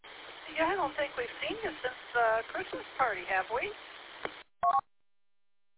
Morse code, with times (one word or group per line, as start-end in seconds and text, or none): none